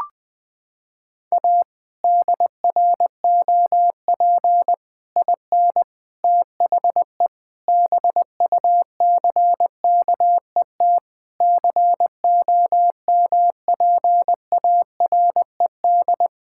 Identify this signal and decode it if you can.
Morse code, heard as a keyed tone, about 21 wpm, 700 Hz. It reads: A DROP IN THE BUCKET COMPARED
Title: Sets of 8 Words - 20wpm